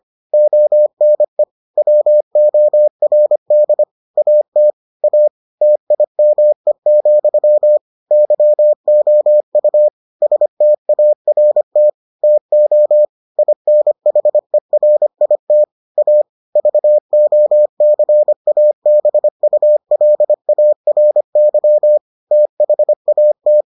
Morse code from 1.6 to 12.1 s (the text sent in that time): WORD AT A TIME, YOU START